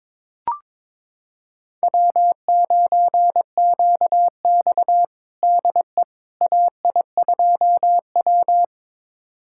Morse code, heard as a keyed tone, about 22 wpm, 700 Hz. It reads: W9QX DE AI2W
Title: USA Callsign DE Callsign - 4 Characters - 22wpm